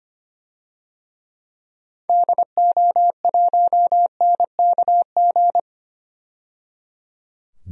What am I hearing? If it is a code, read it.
Morse code, heard as a keyed tone, about 25 wpm, 700 Hz. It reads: DO1NKG
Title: DXCC-100 Call Signs - 6 Characters - 25wpm